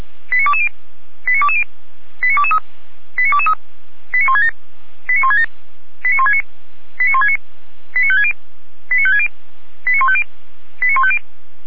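Five-tone selective call (ZVEI), heard as quick runs of five tones, tones of about 70 ms. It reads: 89220 89220 89223 89223 89167 89167 89169 89169 89560 89560 89150 89150
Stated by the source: Received on an SDR as a ZVEI five-tone call.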